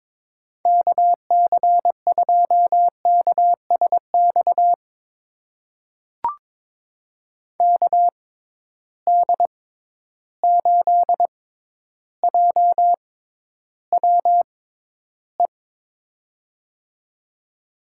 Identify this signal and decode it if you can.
Morse code, heard as a keyed tone, about 22 wpm, spaced as for 11 wpm, 700 Hz. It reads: KC2KSX KD8JWE
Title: USA Call Signs - 6 Characters - 6x Character Spacing - 22wpm